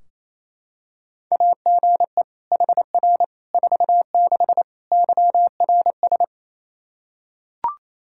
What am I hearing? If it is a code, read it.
Morse code, heard as a keyed tone, about 28 wpm, 700 Hz. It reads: AGE HR 46 YRS
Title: Practice QSO - Exchange 3 - More Info - 28wpm